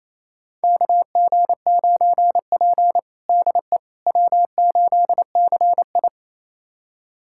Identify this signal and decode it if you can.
Morse code, heard as a keyed tone, about 28 wpm, 700 Hz. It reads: KG9P DE W8CI
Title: USA Callsign DE Callsign - 4 Characters - 28wpm